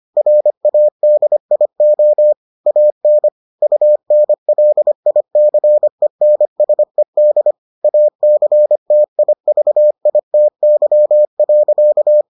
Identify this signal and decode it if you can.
Morse code, heard as a keyed tone, about 25 wpm, 600 Hz. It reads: RADIO AN UNLICENSED ACTIVITY.